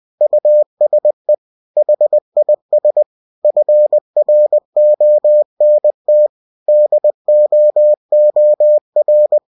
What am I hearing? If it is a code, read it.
Morse code, heard as a keyed tone, about 20 wpm, 600 Hz. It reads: USE HIS FRONT DOOR
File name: cw-books / winnie-the-pooh / winnie-20wpm-page-0018